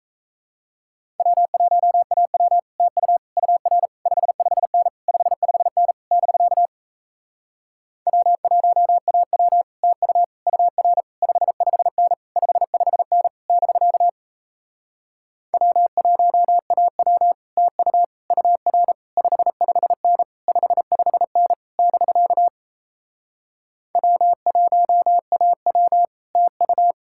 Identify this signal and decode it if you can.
Morse code, heard as a keyed tone, about 37 wpm, 700 Hz. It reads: W1AW TU UR 55N 55N <BK> W1AW TU UR 55N 55N <BK> W1AW TU UR 55N 55N <BK> W1AW TU